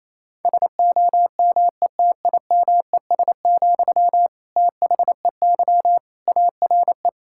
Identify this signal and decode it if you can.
Morse code, heard as a keyed tone, about 28 wpm, 700 Hz. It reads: SOMETIMES, THEY ARE
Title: Sentences from Top 1000 Words - Review Part 5 - 28wpm